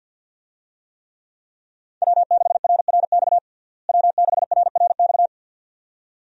Morse code, heard as a keyed tone, about 50 wpm, 700 Hz. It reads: W6RRX W6RRX
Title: USA Call Signs - 5 Characters - Repeated 2x - 50wpm